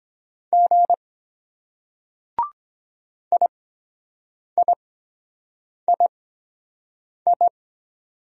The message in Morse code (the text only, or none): G IIII